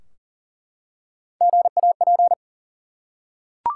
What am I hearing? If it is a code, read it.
Morse code, heard as a keyed tone, about 42 wpm, 700 Hz. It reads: GAP
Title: Three Letter Words - 40wpm